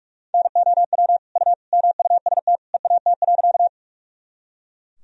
Morse code, heard as a keyed tone, about 43 wpm, 700 Hz. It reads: NOW U MUST EAT.
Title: Sentences from Top 300 Words and Abbreviations 45wpm